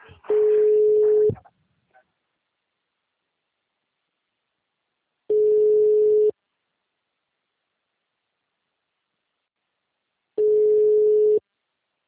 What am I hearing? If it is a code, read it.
Ringback tone (Europe, 425 Hz).